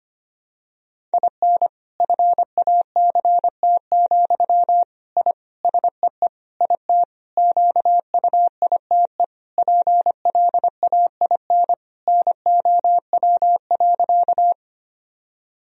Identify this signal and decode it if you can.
Morse code, heard as a keyed tone, about 25 wpm, 700 Hz. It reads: IN FACT, I SEE IT QUITE PLAIN NOW.
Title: Sentences from Top 1000 Words - Review Part 3 - 25wpm